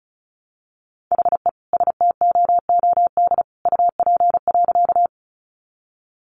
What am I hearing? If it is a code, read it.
Morse code, heard as a keyed tone, about 34 wpm, 700 Hz. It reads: HE STOOD UP.
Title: Sentences from Top 500 Words 35wpm